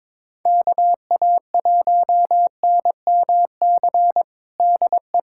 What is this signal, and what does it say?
Morse code, heard as a keyed tone, about 22 wpm, 700 Hz. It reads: KA1NMC DE
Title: USA Callsign DE Callsign 22wpm